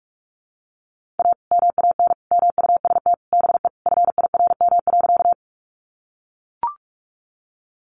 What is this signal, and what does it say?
Morse code, heard as a keyed tone, about 44 wpm, 700 Hz. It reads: A MAN MUST BE FIRM.
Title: Sentences from Top 1400 Words 45wpm